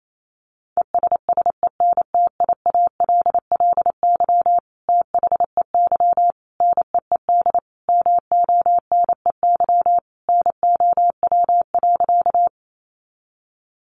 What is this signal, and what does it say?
Morse code, heard as a keyed tone, about 28 wpm, 700 Hz. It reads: ESSENTIALLY THEY NEED MONEY NOW.
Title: Sentences from Top 1300 Words 28wpm